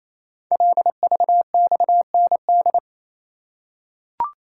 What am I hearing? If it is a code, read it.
Morse code, heard as a keyed tone, about 29 wpm, 690 Hz. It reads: LVXND
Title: Code Groups - 5 Letters - 28wpm